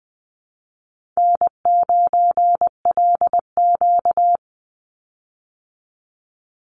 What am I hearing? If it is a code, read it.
Morse code, heard as a keyed tone, about 20 wpm, 700 Hz. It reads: N9LQ
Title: Easy Call Signs - 20wpm